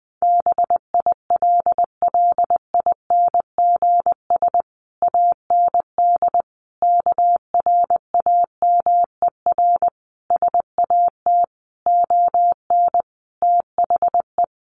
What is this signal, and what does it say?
Morse code, heard as a keyed tone, about 20 wpm, 700 Hz. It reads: BILLINGS AND KRAMER SAT ON THE
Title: A Sound Of Thunder 20wpm